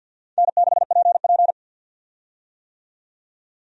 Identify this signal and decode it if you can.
Morse code, heard as a keyed tone, about 52 wpm, 700 Hz. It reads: N6PP
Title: USA Call Signs - 4 Characters - 50wpm